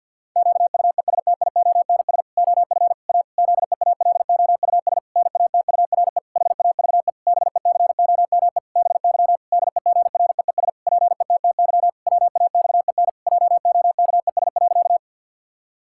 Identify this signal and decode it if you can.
Morse code, heard as a keyed tone, about 50 wpm, 700 Hz. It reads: QUESTIONS OF A ZEALOUS NATURE HAVE BECOME BY DEGREES PETTY WAXEN JOKES.
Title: Pangram Challenge v2 - 50wpm